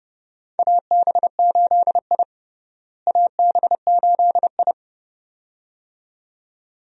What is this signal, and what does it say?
Morse code, heard as a keyed tone, about 32 wpm, 700 Hz. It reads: AB8I AB8I
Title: USA Call Signs - 4 Characters - Repeated 2x - 30wpm